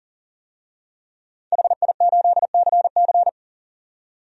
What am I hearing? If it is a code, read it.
Morse code, heard as a keyed tone, about 40 wpm, 700 Hz. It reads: HI8CC
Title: DXCC-100 Call Signs - 5 Characters - 40wpm